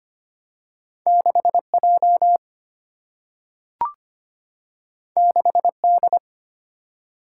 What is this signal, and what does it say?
Morse code, heard as a keyed tone, about 25 wpm, 700 Hz. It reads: 6J 6D
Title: Single-Letter Single-Number 25wpm